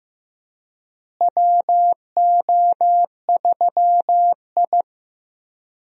Morse code, heard as a keyed tone, about 15 wpm, 700 Hz. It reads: WO3I
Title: USA Call Signs - 4 Characters - Repeated 3x - 15wpm